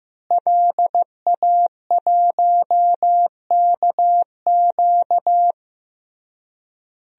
Morse code, heard as a keyed tone, about 15 wpm, 700 Hz. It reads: LA1KQ
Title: DXCC-100 Call Signs - 5 Characters - Repeated 3x - 15wpm